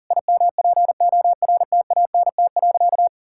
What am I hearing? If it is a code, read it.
Morse code, heard as a keyed tone, about 37 wpm, 700 Hz. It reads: IMPORTANT.
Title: Sentences from Top 1000 Words - Review Part 6 - 40wpm